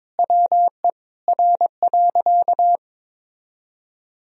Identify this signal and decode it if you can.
Morse code, heard as a keyed tone, about 21 wpm, 700 Hz. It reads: WE R.
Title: Sentences from Top 200 Words and Abbreviations 22wpm